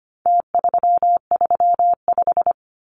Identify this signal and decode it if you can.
Morse code, heard as a keyed tone, about 25 wpm, 700 Hz. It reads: T335
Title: Numbers - Contest Style - 4-Digit 25wpm